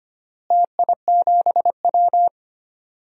Morse code, heard as a keyed tone, about 25 wpm, 700 Hz. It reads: TI7W
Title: DXCC-100 Call Signs - 4 Characters - Repeated 3x - 25wpm